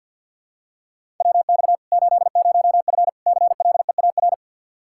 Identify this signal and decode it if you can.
Morse code, heard as a keyed tone, about 52 wpm, 700 Hz. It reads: WX 80F CLEAR
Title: Practice QSO - Exchange 2 - Info - 50wpm